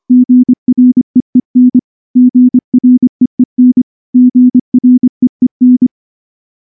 Morse code, heard as a keyed tone, about 23 wpm, 250 Hz. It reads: GREEN GREEN GREEN
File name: morse-training-25WPM